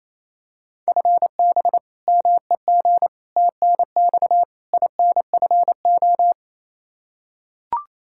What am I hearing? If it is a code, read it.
Morse code, heard as a keyed tone, about 28 wpm, 700 Hz. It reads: FB MEG TNX INFO
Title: Practice QSO - Exchange 3 - More Info - 28wpm